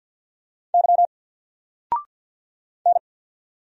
Morse code, heard as a keyed tone, about 48 wpm, 700 Hz. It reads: Y N
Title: Single Letter-Number Alt 50wpm